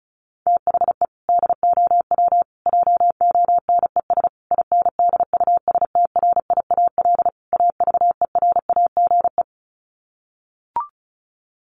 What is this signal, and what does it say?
Morse code, heard as a keyed tone, about 34 wpm, 700 Hz. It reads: THE DOW JONES INDUSTRIAL AVERAGE
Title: Sets of 5 Words 35wpm